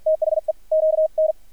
Morse code, heard as a keyed tone, about 53 wpm, spaced as for 46 wpm, 630 Hz, with radio noise. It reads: THE QN
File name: archived / CRSnet